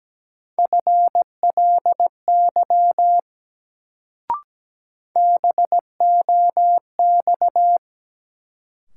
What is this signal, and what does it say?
Morse code, heard as a keyed tone, about 17 wpm, 700 Hz. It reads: FLY BOX
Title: Sentences from Top 500 Words - New Word Drill - 17wpm